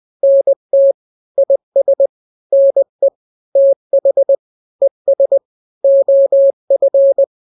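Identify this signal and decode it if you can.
Morse code, heard as a keyed tone, about 20 wpm, 550 Hz. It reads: NT IS NE TH ES OF
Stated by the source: Morse drill of two-letter n-grams at 20/18 WPM (Farnsworth), from KK7RR.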